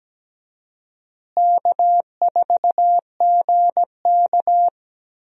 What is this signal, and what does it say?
Morse code, heard as a keyed tone, about 17 wpm, 700 Hz. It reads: K4GK
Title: USA Call Signs - 4 Characters - Repeated 3x - 17wpm